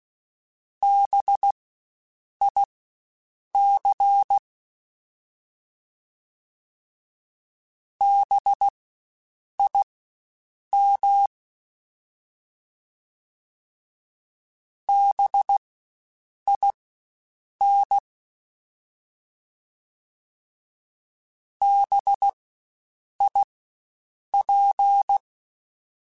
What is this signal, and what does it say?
Morse code, heard as a keyed tone, about 16 wpm, spaced as for 7 wpm, 800 Hz. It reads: BIC BIM BIN BIP
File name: Spaced Alphas / Spaced Alphas mp3s / Bs sp @15